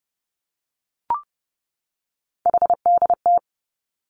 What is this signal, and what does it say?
Morse code, heard as a keyed tone, about 30 wpm, 700 Hz. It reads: HDT
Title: Three Challenging Letter-Number Combinations 30wpm